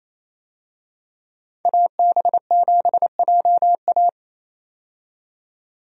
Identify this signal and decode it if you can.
Morse code, heard as a keyed tone, about 28 wpm, 700 Hz. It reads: AB7JA
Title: USA Call Signs - 5 Characters - 28wpm